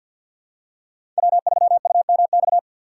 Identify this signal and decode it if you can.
Morse code, heard as a keyed tone, about 53 wpm, 700 Hz. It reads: W3UMX
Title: USA Call Signs - 5 Characters - 4x Character Spacing - 50wpm